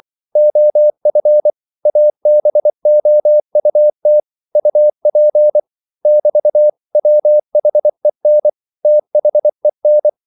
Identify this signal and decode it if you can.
Morse code, heard as a keyed tone, about 24 wpm, 600 Hz. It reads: OF ABOUT UP <BT> WHEN THEN
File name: CW / Practice / 100MostCommonEnglishWords24